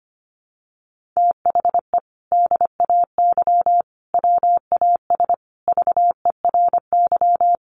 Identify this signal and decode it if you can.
Morse code, heard as a keyed tone, about 25 wpm, 700 Hz. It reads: THE DAY WAS VERY
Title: Sentences from Top 1500 Words 25wpm